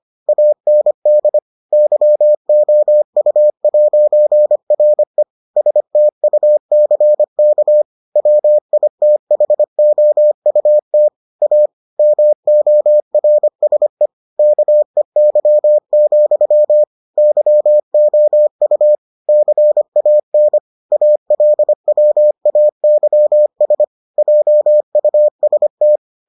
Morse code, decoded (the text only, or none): AND YOU'RE STUCK WITHOUT A MORSE KEY, YOU CAN ALWAYS JUST